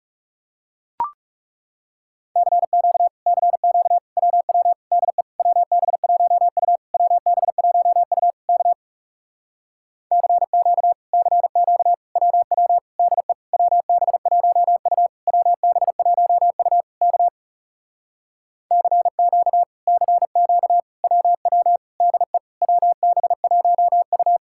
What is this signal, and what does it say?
Morse code, heard as a keyed tone, about 40 wpm, 700 Hz. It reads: CQ CQ WW DE WB1U WB1U K CQ CQ WW DE WB1U WB1U K CQ CQ WW DE WB1U